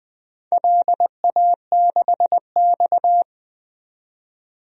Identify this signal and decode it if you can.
Morse code, heard as a keyed tone, about 20 wpm, 700 Hz. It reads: LA6X
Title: DXCC-100 Call Signs - 4 Characters - 20wpm